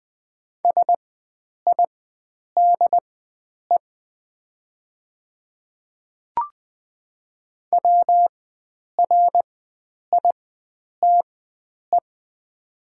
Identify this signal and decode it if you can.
Morse code, heard as a keyed tone, about 20 wpm, spaced as for 9 wpm, 690 Hz. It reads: SIDE WRITE
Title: 4x Character Spacing - Top 100 Words - 20wpm